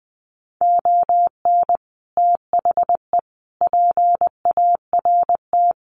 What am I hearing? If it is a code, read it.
Morse code, heard as a keyed tone, about 20 wpm, 700 Hz. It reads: ON THE PART
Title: Sets of 5 Words 20wpm